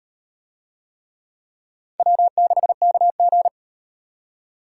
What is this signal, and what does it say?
Morse code, heard as a keyed tone, about 38 wpm, 700 Hz. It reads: W6KG